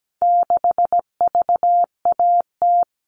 Morse code, heard as a keyed tone, about 17 wpm, 700 Hz. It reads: 6VAT
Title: All Cut Numbers - 4-Digit 17wpm